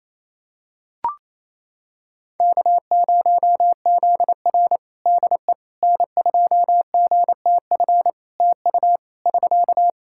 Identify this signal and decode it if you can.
Morse code, heard as a keyed tone, about 28 wpm, 700 Hz. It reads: K0ZR DE N2GTF TU <SK>
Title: Practice QSO - Exchange 4 - Ending - 28wpm